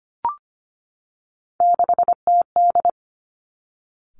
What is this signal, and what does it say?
Morse code, heard as a keyed tone, about 25 wpm, 700 Hz. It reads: 6TD
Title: Three Challenging Letter-Number Combinations 25wpm